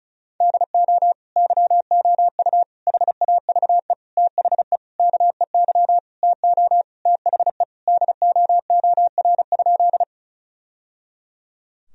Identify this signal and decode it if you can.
Morse code, heard as a keyed tone, about 34 wpm, 700 Hz. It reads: DO YOU HAVE THE KEY TO THE DOOR?
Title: Sentences from Top 1000 Words - Review Part 4 - 35wpm